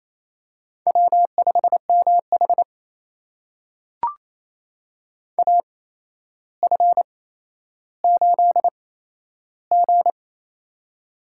Morse code, heard as a keyed tone, about 28 wpm, spaced as for 12 wpm, 700 Hz. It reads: W5MH AF8G
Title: USA Call Signs - 4 Characters - 8x Character Spacing - 28wpm